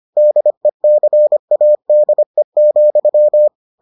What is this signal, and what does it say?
Morse code, heard as a keyed tone, about 25 wpm, 600 Hz. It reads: DECADE,